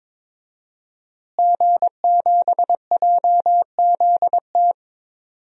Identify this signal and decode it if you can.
Morse code, heard as a keyed tone, about 21 wpm, 700 Hz. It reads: G7JZT
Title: DXCC-100 Call Signs - 5 Characters - Repeated 2x - 22wpm